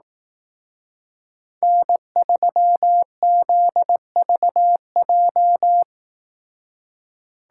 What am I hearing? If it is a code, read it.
Morse code, heard as a keyed tone, about 18 wpm, 700 Hz. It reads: N3ZVJ
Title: Speed Racing - USA 1x3 Call Signs 15wpm